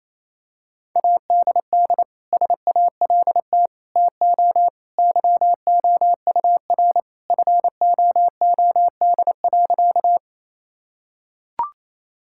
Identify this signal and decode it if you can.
Morse code, heard as a keyed tone, about 28 wpm, 700 Hz. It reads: ADD SALT TO YOUR FOOD.
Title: Sentences from Top 1000 Words - Review Part 5 - 28wpm